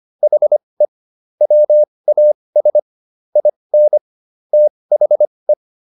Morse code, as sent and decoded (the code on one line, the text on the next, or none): .... / .-- .- ... / .. -. / - .... .
HE WAS IN THE